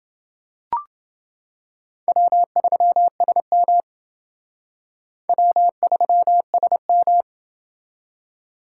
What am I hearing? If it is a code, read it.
Morse code, heard as a keyed tone, about 28 wpm, 700 Hz. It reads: W3SM W3SM